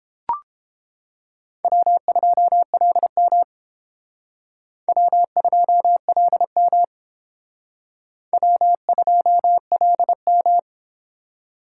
Morse code, heard as a keyed tone, about 31 wpm, 700 Hz. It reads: W2LM W2LM W2LM